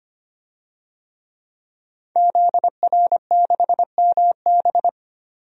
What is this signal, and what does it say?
Morse code, heard as a keyed tone, about 25 wpm, 700 Hz. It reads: ZR6MB
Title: DXCC-100 Call Signs - 5 Characters - 25wpm